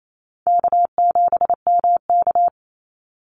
Morse code, heard as a keyed tone, about 28 wpm, 700 Hz. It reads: K7MK
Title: NA SOTA Activator Callsigns 28wpm